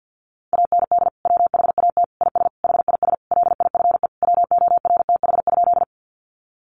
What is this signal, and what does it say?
Morse code, heard as a keyed tone, about 50 wpm, 700 Hz. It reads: AND WHAT IS HIS LIFE WORTH?
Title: Sentences from Top 800 Words 50wpm